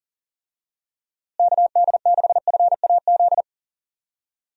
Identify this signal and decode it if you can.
Morse code, heard as a keyed tone, about 40 wpm, 700 Hz. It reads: KD6FAZ
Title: USA Call Signs - 6 Characters - Repeated 2x - 40wpm